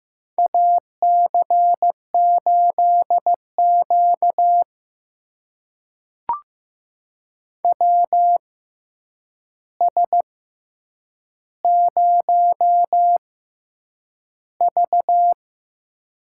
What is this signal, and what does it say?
Morse code, heard as a keyed tone, about 15 wpm, spaced as for 8 wpm, 700 Hz. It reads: AC8Q WS0V
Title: USA Call Signs - 4 Characters - 6x Character Spacing - 15wpm